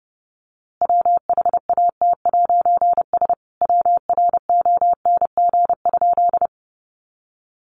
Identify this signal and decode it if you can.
Morse code, heard as a keyed tone, about 30 wpm, 700 Hz. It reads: WHAT'S WRONG?
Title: A Sound Of Thunder 30wpm